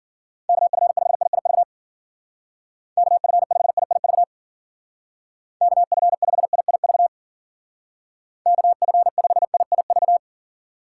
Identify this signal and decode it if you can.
Morse code, heard as a keyed tone, about 50 wpm, 700 Hz. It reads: KF5IIV KF5IIV KF5IIV KF5IIV
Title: Speed Racing - USA 2x3 Call Signs 40wpm